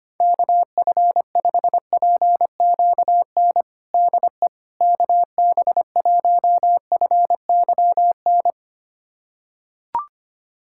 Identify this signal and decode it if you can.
Morse code, heard as a keyed tone, about 25 wpm, 700 Hz. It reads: KF5PQN DE KB1FYN
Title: USA Callsign DE Callsign - 6 Characters - 25wpm